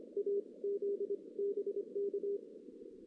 Morse code, heard as a keyed tone, about 26 wpm, 400 Hz. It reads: A Z BK
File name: 2016 JAN CW NAQP